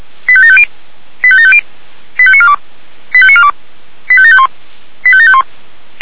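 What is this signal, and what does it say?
Five-tone selective call (ZVEI), heard as quick runs of five tones, tones of about 70 ms. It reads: 85750 85750 85942 85942 85751 85751
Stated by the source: Five-tone ZVEI selective call received by SDR, logged by the fire brigade (Feuerwehr) Orlamünde.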